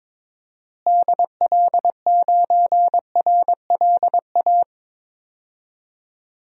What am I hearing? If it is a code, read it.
Morse code, heard as a keyed tone, about 21 wpm, 700 Hz. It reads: DL9RLA
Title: DXCC-100 Call Signs - 6 Characters - Repeated 3x - 22wpm